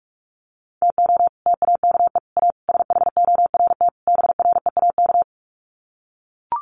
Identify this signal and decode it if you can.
Morse code, heard as a keyed tone, about 44 wpm, 700 Hz. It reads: TO TAKE A SHORT BREAK
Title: Sets of 5 Words Encore2 45wpm